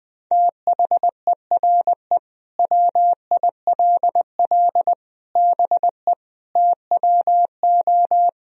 Morse code, heard as a keyed tone, about 20 wpm, 700 Hz. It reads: THERE WILL BE TWO
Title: Sentences from Top 1000 Words - Review Part 3 - 20wpm